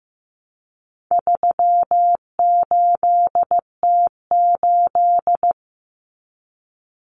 Morse code, heard as a keyed tone, about 15 wpm, 700 Hz. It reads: 38T8
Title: Common Cut Numbers - 4-Digit 15wpm